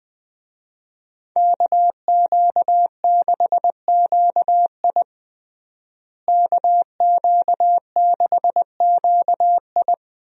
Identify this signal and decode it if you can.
Morse code, heard as a keyed tone, about 21 wpm, 700 Hz. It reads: KQ6QI KQ6QI
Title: USA Call Signs - 5 Characters - Repeated 2x - 20wpm